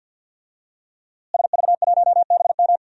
Morse code, heard as a keyed tone, about 53 wpm, 700 Hz. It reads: SV1BM
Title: DXCC-100 Call Signs - 5 Characters - Repeated 3x - 50wpm